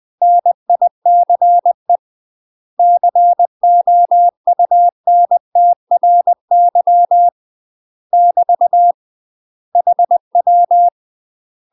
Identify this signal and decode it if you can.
Morse code, heard as a keyed tone, about 20 wpm, 700 Hz. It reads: NICE COUNTRY <BT> HW
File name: UK MASTER QSOs TRAINING / 20WPM / UK MASTER QSOs 0014